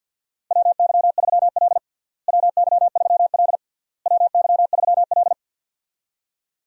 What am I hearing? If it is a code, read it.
Morse code, heard as a keyed tone, about 52 wpm, 700 Hz. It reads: WY3L WY3L WY3L
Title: USA Call Signs - 4 Characters - Repeated 3x - 50wpm